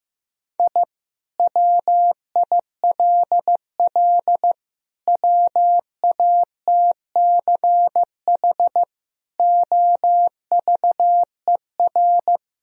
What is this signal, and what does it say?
Morse code, heard as a keyed tone, about 16 wpm, 700 Hz. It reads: I WILL WATCH OVER
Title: Sentences from Top 1000 Words - Review Part 1 - 15wpm